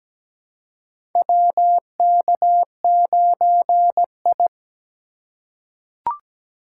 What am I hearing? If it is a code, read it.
Morse code, heard as a keyed tone, about 17 wpm, 700 Hz. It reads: WK9I
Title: USA Call Signs - 4 Characters - 2x Character Spacing - 17wpm